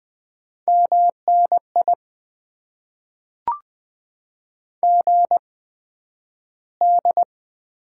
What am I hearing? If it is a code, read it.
Morse code, heard as a keyed tone, about 20 wpm, spaced as for 11 wpm, 700 Hz. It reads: MNI GD
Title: Top 90 QSO Elements - 8x Character Spacing - 20wpm